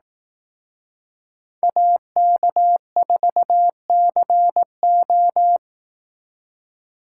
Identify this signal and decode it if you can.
Morse code, heard as a keyed tone, about 18 wpm, 700 Hz. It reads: AK4CO
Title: Speed Racing - USA 2x2 Call Signs 15wpm